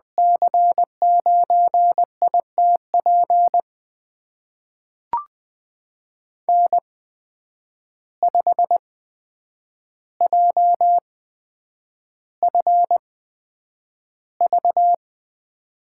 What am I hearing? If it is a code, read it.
Morse code, heard as a keyed tone, about 20 wpm, spaced as for 9 wpm, 700 Hz. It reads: C9ITP N5JFV
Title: USA Call Signs - 8x Character Spacing - 20wpm